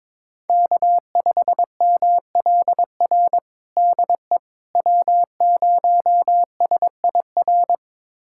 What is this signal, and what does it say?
Morse code, heard as a keyed tone, about 22 wpm, 700 Hz. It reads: K5MLR DE W0SIR
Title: USA Callsign DE Callsign - 5 Characters - 22wpm